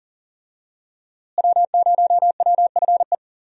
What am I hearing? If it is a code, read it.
Morse code, heard as a keyed tone, about 40 wpm, 700 Hz. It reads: W0WFE
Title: USA Call Signs - 5 Characters - Repeated 3x - 40wpm